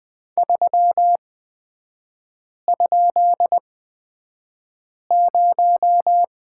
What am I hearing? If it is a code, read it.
Morse code, heard as a keyed tone, about 20 wpm, spaced as for 5 wpm, 700 Hz. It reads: 3?0
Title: 255 Lesson 41e - Period - Three Characters 20wpm